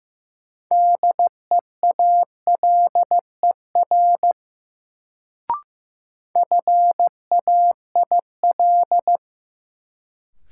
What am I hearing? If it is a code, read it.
Morse code, heard as a keyed tone, about 16 wpm, 700 Hz. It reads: DEALER FAIL
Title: Sentences from Top 1100 Words - New Word Drill Alt - 15wpm